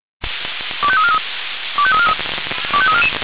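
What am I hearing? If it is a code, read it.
Five-tone selective call (ZVEI), heard as quick runs of five tones, tones of about 70 ms. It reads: 25343 25343 25344
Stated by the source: ZVEI five-tone selective call 25220 received on SDR.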